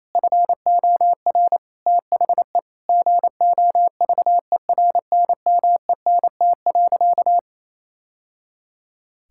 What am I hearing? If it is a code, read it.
Morse code, heard as a keyed tone, about 28 wpm, 700 Hz. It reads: FOR THE GOVERNMENT.
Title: Sentences from Top 1000 Words - Review Part 6 - 28wpm